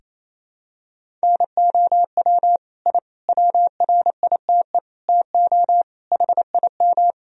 Morse code, heard as a keyed tone, about 28 wpm, 700 Hz. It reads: NOW I WRITE TO HIM
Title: Sentences from Top 1000 Words - Review Part 1 - 28wpm